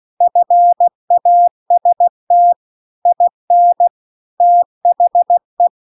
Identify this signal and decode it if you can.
Morse code, heard as a keyed tone, about 16 wpm, 700 Hz. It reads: FAST IN THE
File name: CW-The-Big-Twoheaded-River-2-16WPM